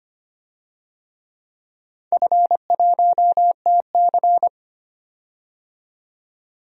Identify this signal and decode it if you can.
Morse code, heard as a keyed tone, about 25 wpm, 700 Hz. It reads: F1TC
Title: DXCC-100 Call Signs - 4 Characters - 25wpm